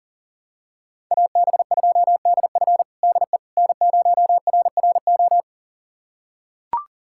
Morse code, heard as a keyed tone, about 37 wpm, 700 Hz. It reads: AB2DF DE N0RRO
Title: USA Callsign DE Callsign 40wpm